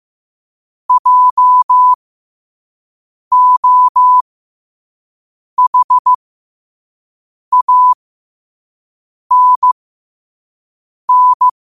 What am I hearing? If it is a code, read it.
Morse code, heard as a keyed tone, about 15 wpm, spaced as for 5 wpm, 1000 Hz. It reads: JOHANN
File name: qotdfiles / 5wpm